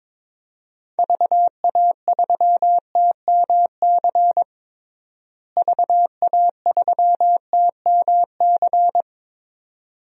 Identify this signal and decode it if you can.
Morse code, heard as a keyed tone, about 21 wpm, 700 Hz. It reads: VA3TMC VA3TMC
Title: DXCC-100 Call Signs - 6 Characters - Repeated 2x - 22wpm